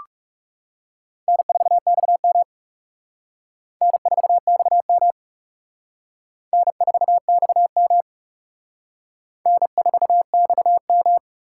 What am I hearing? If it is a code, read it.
Morse code, heard as a keyed tone, about 36 wpm, 700 Hz. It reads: N4XM N4XM N4XM N4XM
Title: Speed Racing - USA 1x2 Call Signs 30wpm